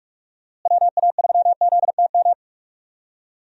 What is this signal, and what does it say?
Morse code, heard as a keyed tone, about 41 wpm, 700 Hz. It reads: WA3ZTM
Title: USA Call Signs - 6 Characters - Repeated 2x - 45wpm